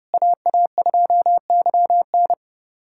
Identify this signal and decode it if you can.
Morse code, heard as a keyed tone, about 32 wpm, 700 Hz. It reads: AA2YN
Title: USA Callsign DE Callsign - 5 Characters - 30wpm